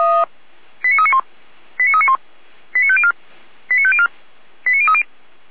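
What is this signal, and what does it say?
Five-tone selective call (ZVEI), heard as quick runs of five tones, tones of about 70 ms. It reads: 89391 89391 89504 89504 89030
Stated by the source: ZVEI five-tone selective call received on an SDR.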